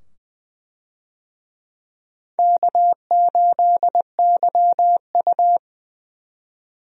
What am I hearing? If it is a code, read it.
Morse code, heard as a keyed tone, about 20 wpm, 700 Hz. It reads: K8YU
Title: USA Call Signs - 4 Characters - 20wpm